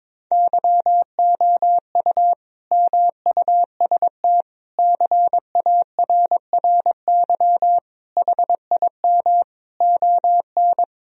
Morse code, heard as a keyed tone, about 21 wpm, 700 Hz. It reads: YOU MUST CARRY HIM ON
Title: Sentences from Top 1000 Words - Review Part 1 - 22wpm